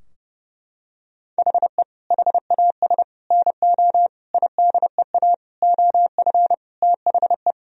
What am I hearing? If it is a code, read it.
Morse code, heard as a keyed tone, about 30 wpm, 700 Hz. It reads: HE HAS NO IDEA OF THE
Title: Sentences from Top 1000 Words - Review Part 3 - 30wpm